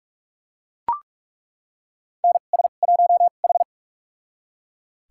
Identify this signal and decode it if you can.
Morse code, heard as a keyed tone, about 42 wpm, spaced as for 32 wpm, 690 Hz. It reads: NS1H
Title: USA Call Signs - 4 Characters - 2x Character Spacing - 45wpm